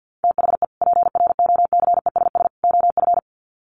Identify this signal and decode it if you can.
Morse code, heard as a keyed tone, about 50 wpm, 700 Hz. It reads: THE PROCESS OF
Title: Sets of 3 Words 50wpm